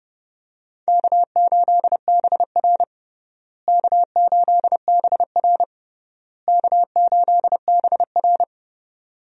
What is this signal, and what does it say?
Morse code, heard as a keyed tone, about 32 wpm, 700 Hz. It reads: K8BR K8BR K8BR
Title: USA Call Signs - 4 Characters - Repeated 3x - 30wpm